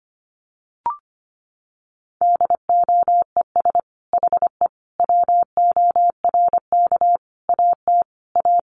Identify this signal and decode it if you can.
Morse code, heard as a keyed tone, about 25 wpm, 700 Hz. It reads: DOES HE WORK AT A
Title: Sentences from Top 1300 Words 25wpm